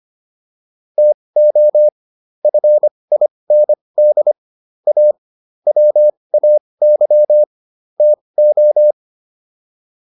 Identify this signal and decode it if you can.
Morse code, heard as a keyed tone, about 25 wpm, spaced as for 20 wpm, 600 Hz. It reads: TO FIND A WAY TO